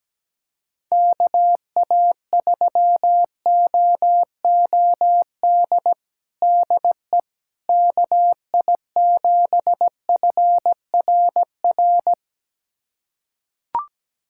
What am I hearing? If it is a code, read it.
Morse code, heard as a keyed tone, about 17 wpm, 700 Hz. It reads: KA3OOD DE KI7FRR
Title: USA Callsign DE Callsign 17wpm